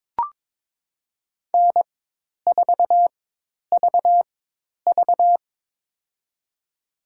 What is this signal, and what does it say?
Morse code, heard as a keyed tone, about 22 wpm, spaced as for 10 wpm, 700 Hz. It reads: N4VV
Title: USA Call Signs - 4 Characters - 4x Character Spacing - 22wpm